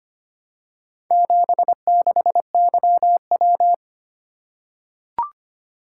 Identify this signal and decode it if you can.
Morse code, heard as a keyed tone, about 25 wpm, 700 Hz. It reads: 76YW
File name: practice / 25wpm / Four Letter-Number Combinations - 25wpm